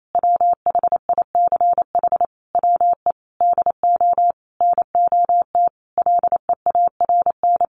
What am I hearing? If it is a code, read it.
Morse code, heard as a keyed tone, about 28 wpm, 700 Hz. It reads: WHICH WE DO NOT LEARN
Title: Sentences from Top 1400 Words 28wpm